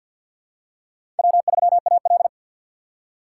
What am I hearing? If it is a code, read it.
Morse code, heard as a keyed tone, about 48 wpm, 700 Hz. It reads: W3AL
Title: USA Call Signs - 4 Characters - 3x Character Spacing - 50wpm